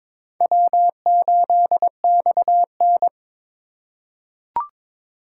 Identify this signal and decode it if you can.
Morse code, heard as a keyed tone, about 22 wpm, 700 Hz. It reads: W8XN
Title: USA Call Signs - 4 Characters - 4x Character Spacing - 22wpm